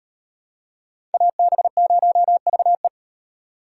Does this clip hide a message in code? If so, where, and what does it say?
Yes: Morse code from 1.1 to 2.9 s, heard as a keyed tone, about 38 wpm, 700 Hz. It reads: AB0VE